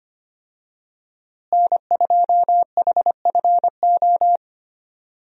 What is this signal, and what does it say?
Morse code, heard as a keyed tone, about 25 wpm, 700 Hz. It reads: N2HFO
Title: USA Call Signs - 5 Characters - Repeated 2x - 25wpm